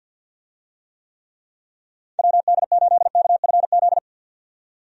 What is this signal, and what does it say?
Morse code, heard as a keyed tone, about 53 wpm, 700 Hz. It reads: WD8KFZ
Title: USA Call Signs - 6 Characters - 50wpm